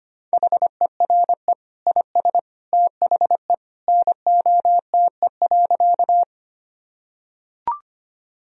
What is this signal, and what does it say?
Morse code, heard as a keyed tone, about 25 wpm, 700 Hz. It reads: HERE IS THE NOTE.
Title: Sentences from Top 1000 Words - Review Part 2 - 25wpm